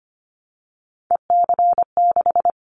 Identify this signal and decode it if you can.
Morse code, heard as a keyed tone, about 25 wpm, 700 Hz. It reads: EC6
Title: Callsign Prefixes for Top 50 DXCC - 25wpm